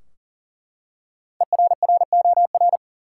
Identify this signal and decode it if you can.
Morse code, heard as a keyed tone, about 40 wpm, 700 Hz. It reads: ERROR
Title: Five Letter Words - 40wpm